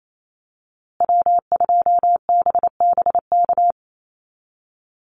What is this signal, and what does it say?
Morse code, heard as a keyed tone, about 28 wpm, 700 Hz. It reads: W2BBK
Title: USA Call Signs Repeated 2x 28wpm